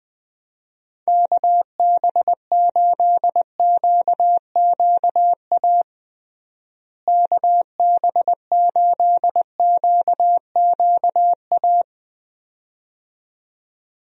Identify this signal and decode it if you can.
Morse code, heard as a keyed tone, about 21 wpm, 700 Hz. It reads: KB8QQA KB8QQA
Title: USA Call Signs - 6 Characters - Repeated 2x - 20wpm